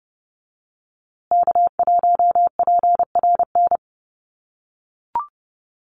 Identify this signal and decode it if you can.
Morse code, heard as a keyed tone, about 30 wpm, 700 Hz. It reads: K1PRN
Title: USA Call Signs Repeated 2x 30wpm